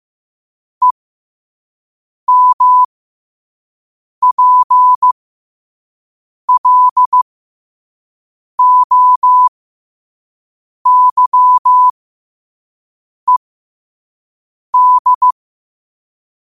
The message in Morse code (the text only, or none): EMPLOYED